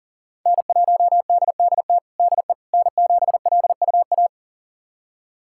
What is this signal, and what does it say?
Morse code, heard as a keyed tone, about 40 wpm, 700 Hz. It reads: N1DDT DE N7LUA
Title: USA Callsign DE Callsign - 5 Characters - 40wpm